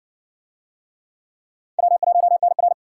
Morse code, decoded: W1NR

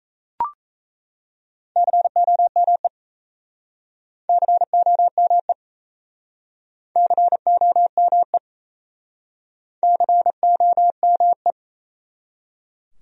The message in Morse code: COME COME COME COME

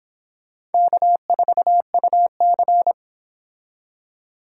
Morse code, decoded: K4UC

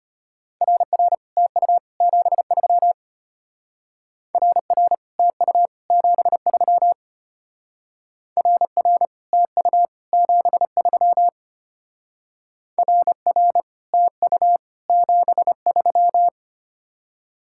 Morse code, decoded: RR TU 73 RR TU 73 RR TU 73 RR TU 73

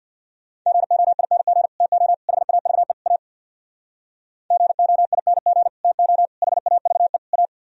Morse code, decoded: GOING TO HAVE A GOING TO HAVE A

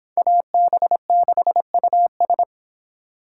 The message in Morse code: AB6US

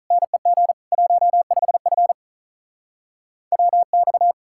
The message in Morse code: NEG 15F WX

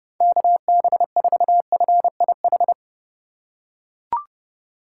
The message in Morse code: KB4FIH

